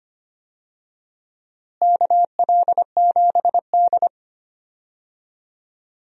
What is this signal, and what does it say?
Morse code, heard as a keyed tone, about 25 wpm, 700 Hz. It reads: KL7D
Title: DXCC-100 Call Signs - 4 Characters - Repeated 3x - 25wpm